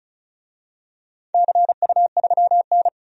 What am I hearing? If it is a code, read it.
Morse code, heard as a keyed tone, about 36 wpm, 700 Hz. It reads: CU3N
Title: DXCC-100 Call Signs - 4 Characters - 35wpm